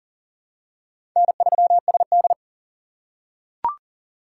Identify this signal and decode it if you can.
Morse code, heard as a keyed tone, about 40 wpm, 700 Hz. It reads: N3SD
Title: USA Call Signs - 4 Characters - 3x Character Spacing - 40wpm